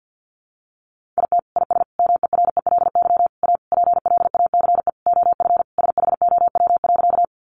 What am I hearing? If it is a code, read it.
Morse code, heard as a keyed tone, about 50 wpm, 700 Hz. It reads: IT IS MERELY A PLACE OF SHOW.
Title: Sentences from Top 1400 Words 50wpm